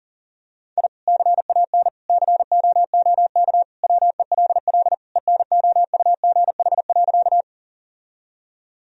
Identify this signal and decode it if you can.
Morse code, heard as a keyed tone, about 40 wpm, 700 Hz. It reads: I CAN COOK WELL ENOUGH.
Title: Sentences from Top 1000 Words - Review Part 4 - 40wpm